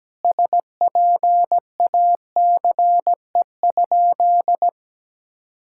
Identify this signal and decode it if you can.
Morse code, heard as a keyed tone, about 17 wpm, 700 Hz. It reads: SPACE?
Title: Sentences from Top 1000 Words - Review Part 5 - 17wpm